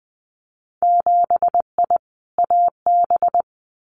Morse code, heard as a keyed tone, about 20 wpm, 700 Hz. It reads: 7I AB
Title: Winter Field Day Exchanges Encore 20wpm